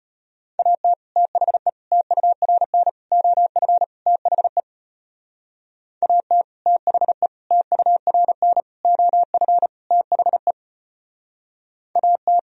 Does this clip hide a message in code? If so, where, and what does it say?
Yes: Morse code from 0.6 to 12.4 s, heard as a keyed tone, about 33 wpm, 700 Hz. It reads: AT THE TURN OF THE AT THE TURN OF THE AT